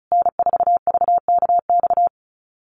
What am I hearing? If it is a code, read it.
Morse code, heard as a keyed tone, about 34 wpm, 700 Hz. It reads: N4VKX
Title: USA Call Signs Repeated 3x 35wpm